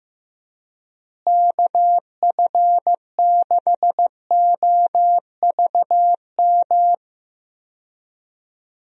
Morse code, heard as a keyed tone, about 15 wpm, 700 Hz. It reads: KF6OVM
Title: USA Call Signs - 6 Characters - Repeated 2x - 15wpm